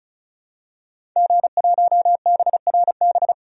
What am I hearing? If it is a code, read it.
Morse code, heard as a keyed tone, about 36 wpm, 700 Hz. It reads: G1BRB
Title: DXCC-100 Call Signs - 5 Characters - 35wpm